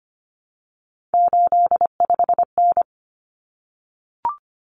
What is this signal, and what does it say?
Morse code, heard as a keyed tone, about 25 wpm, 700 Hz. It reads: 85N
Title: Common Cut Numbers - 3-Digit 25wpm